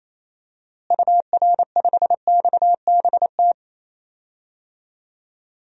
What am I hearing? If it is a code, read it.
Morse code, heard as a keyed tone, about 28 wpm, 700 Hz. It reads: UR5XBT
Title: DXCC-100 Call Signs - 6 Characters - Repeated 3x - 28wpm